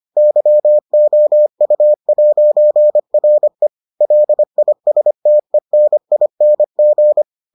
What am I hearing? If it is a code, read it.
Morse code, heard as a keyed tone, about 25 wpm, 600 Hz. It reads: YOU'RE LISTENING